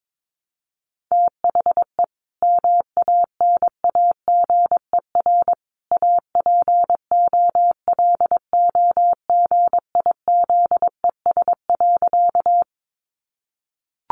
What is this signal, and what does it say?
Morse code, heard as a keyed tone, about 22 wpm, 700 Hz. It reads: THE MANAGER APOLOGIZES.
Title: Additional Easy Phrases 22wpm